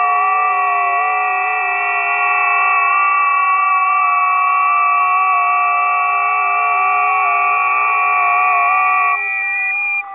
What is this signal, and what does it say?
FT8.
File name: tests / 20m busy / test 03